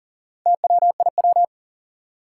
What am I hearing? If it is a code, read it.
Morse code, heard as a keyed tone, about 38 wpm, 700 Hz. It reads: TWIW
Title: Four Challenging Letter-Number Combinations 40wpm